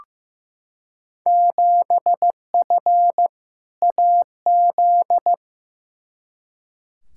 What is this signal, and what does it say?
Morse code, heard as a keyed tone, about 15 wpm, 700 Hz. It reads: 7F AZ
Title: Field Day Exchanges Encore - 15wpm